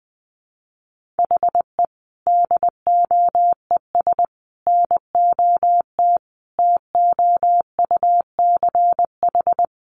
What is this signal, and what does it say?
Morse code, heard as a keyed tone, about 20 wpm, 700 Hz. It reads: HE DOES NOT TOUCH
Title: Sentences from Top 1400 Words 20wpm